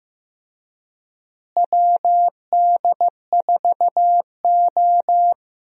Morse code, heard as a keyed tone, about 15 wpm, 700 Hz. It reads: WD4O
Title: USA Call Signs - 4 Characters - Repeated 3x - 15wpm